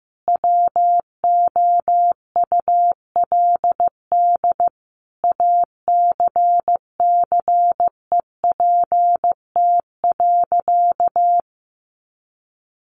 Morse code, heard as a keyed tone, about 15 wpm, 700 Hz. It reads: WOULD ACCEPT.
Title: Sentences from Top 1300 Words 15wpm